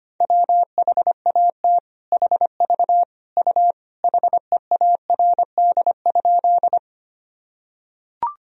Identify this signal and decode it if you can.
Morse code, heard as a keyed tone, about 25 wpm, 700 Hz. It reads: WHAT HV U HEARD?
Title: Sentences from Top 400 Words and Abbreviations 25wpm